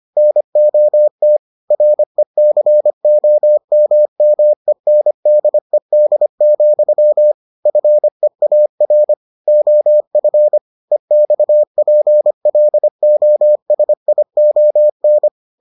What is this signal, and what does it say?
Morse code, heard as a keyed tone, about 25 wpm, 600 Hz. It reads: NOT RECOMMENDED, FEAR OF EXPLOSION